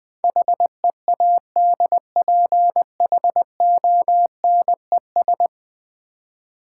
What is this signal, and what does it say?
Morse code, heard as a keyed tone, about 20 wpm, 700 Hz. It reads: HEADPHONES
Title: Easy Words with Hints - 20wpm